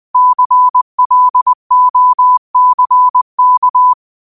Morse code, heard as a keyed tone, about 20 wpm, 1000 Hz. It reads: CLOCK
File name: qotdfiles / 20wpm